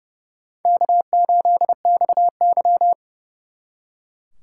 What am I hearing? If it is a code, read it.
Morse code, heard as a keyed tone, about 38 wpm, 700 Hz. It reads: K8XY